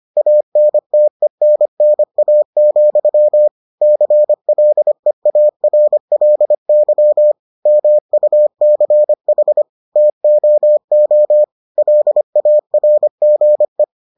Morse code, heard as a keyed tone, about 25 wpm, 600 Hz. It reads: ANTENNA, CLEARLY MUCH TOO LARGE